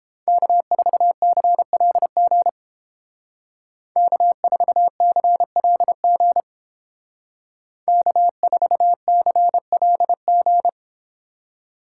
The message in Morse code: K4CLG K4CLG K4CLG

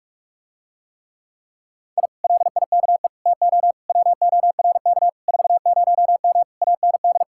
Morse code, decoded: I LIKE TO WORK 40M AND

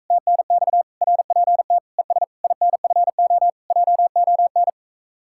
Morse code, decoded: TNX RPT ES INFO JON

